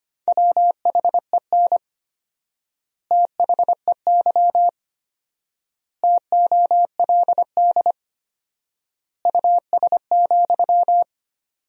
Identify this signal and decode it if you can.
Morse code, heard as a keyed tone, about 25 wpm, 700 Hz. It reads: WHEN THEY TOLD US,
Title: 4x Word-Spacing - Sentences from Top 400 Words - 25wpm